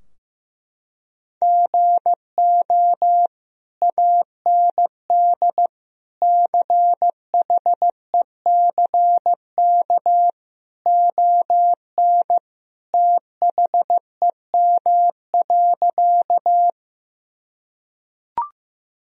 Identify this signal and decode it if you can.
Morse code, heard as a keyed tone, about 15 wpm, 700 Hz. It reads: GO AND CHECK ON THEM.
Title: Sentences from Top 1000 Words - Review Part 3 - 15wpm